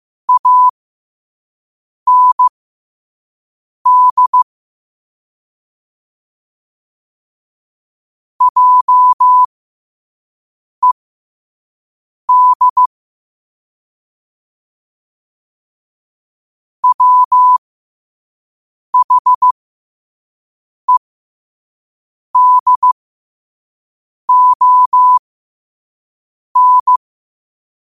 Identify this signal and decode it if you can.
Morse code, heard as a keyed tone, about 15 wpm, spaced as for 5 wpm, 1000 Hz. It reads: AND JED WHEDON
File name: qotdfiles / 5wpm